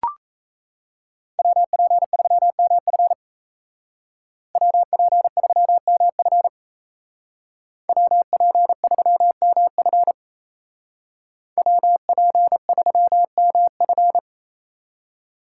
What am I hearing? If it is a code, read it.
Morse code, heard as a keyed tone, about 33 wpm, 700 Hz. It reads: WP3MF WP3MF WP3MF WP3MF
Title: Speed Racing - USA 2x2 Call Signs 28wpm